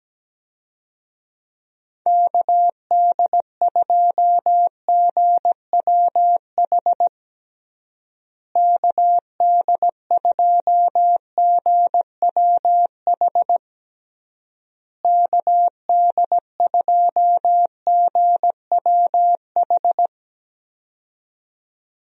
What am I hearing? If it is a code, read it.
Morse code, heard as a keyed tone, about 17 wpm, 700 Hz. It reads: KD2GWH KD2GWH KD2GWH
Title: USA Call Signs - 6 Characters - Repeated 3x - 17wpm